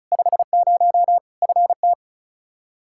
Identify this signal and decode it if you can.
Morse code, heard as a keyed tone, about 36 wpm, 700 Hz. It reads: 50 FT
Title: Practice QSO - Exchange 2 - Info - 35wpm